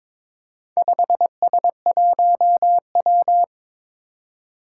Morse code, heard as a keyed tone, about 22 wpm, 700 Hz. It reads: 5S1W
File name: practice / 22wpm / Four Letter-Number Combinations - 22wpm